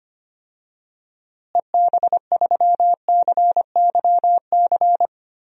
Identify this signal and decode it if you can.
Morse code, heard as a keyed tone, about 25 wpm, 700 Hz. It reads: EB3CYC
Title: DXCC-100 Call Signs - 6 Characters - Repeated 3x - 25wpm